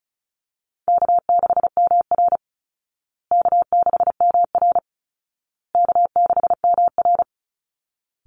Morse code, heard as a keyed tone, about 34 wpm, 700 Hz. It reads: K6MR K6MR K6MR
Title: Easy Call Signs Repeated 3x - 35wpm